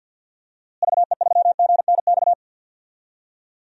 Morse code, heard as a keyed tone, about 53 wpm, 700 Hz. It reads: VE3GNX
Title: DXCC-100 Call Signs - 6 Characters - Repeated 2x - 50wpm